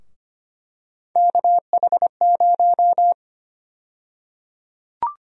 Morse code, heard as a keyed tone, about 25 wpm, 700 Hz. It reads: KH0
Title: USA Call Sign Prefixes - Repeated 2x - 25wpm